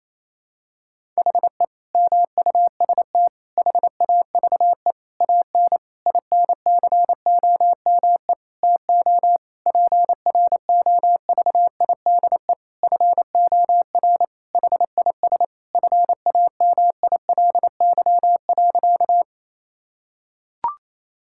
Morse code, heard as a keyed tone, about 28 wpm, 700 Hz. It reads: HE MUST HAVE AN INCOME TO PROVIDE FOR HIS FAMILY.
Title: Sentences from Top 1000 Words - Review Part 5 - 28wpm